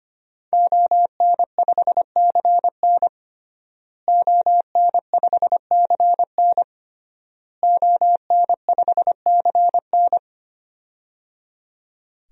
Morse code, heard as a keyed tone, about 25 wpm, 700 Hz. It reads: ON5CN ON5CN ON5CN
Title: DXCC-100 Call Signs - 5 Characters - Repeated 3x - 25wpm